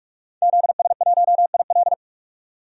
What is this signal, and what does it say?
Morse code, heard as a keyed tone, about 46 wpm, 700 Hz. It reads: ZS1IL
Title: DXCC-100 Call Signs - 5 Characters - 45wpm